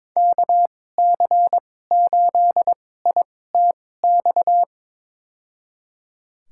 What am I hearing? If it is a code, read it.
Morse code, heard as a keyed tone, about 22 wpm, spaced as for 16 wpm, 700 Hz. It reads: KC8ITX
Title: USA Call Signs - 6 Characters - 2x Character Spacing - 22wpm